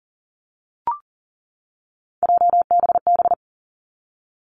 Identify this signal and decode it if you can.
Morse code, heard as a keyed tone, about 41 wpm, 700 Hz. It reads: JBB
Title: Three Challenging Letter-Number Combinations 40wpm